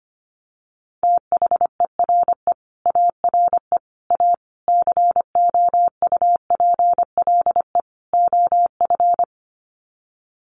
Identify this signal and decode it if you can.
Morse code, heard as a keyed tone, about 25 wpm, 700 Hz. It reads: THERE ARE A COUPLE OF
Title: Sets of 5 Words Encore 25wpm